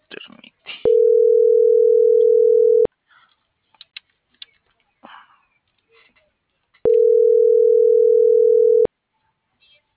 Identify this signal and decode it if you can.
Ringback tone (North America), 440 + 480 Hz.